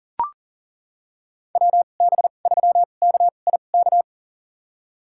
Morse code, heard as a keyed tone, about 40 wpm, spaced as for 29 wpm, 700 Hz. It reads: WB3KIK
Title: USA Call Signs - 2x Character Spacing - 40wpm